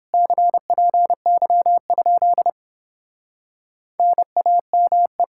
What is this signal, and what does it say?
Morse code, heard as a keyed tone, about 29 wpm, 700 Hz. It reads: CPY? NAME